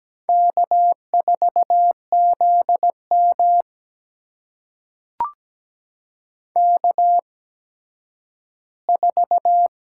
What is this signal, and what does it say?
Morse code, heard as a keyed tone, about 17 wpm, spaced as for 10 wpm, 700 Hz. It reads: K4ZM K4
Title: USA Call Signs - 4 Characters - 8x Character Spacing - 17wpm